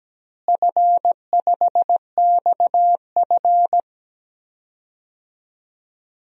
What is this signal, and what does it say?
Morse code, heard as a keyed tone, about 17 wpm, 700 Hz. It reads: F5XF
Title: DXCC-100 Call Signs - 4 Characters - Repeated 3x - 17wpm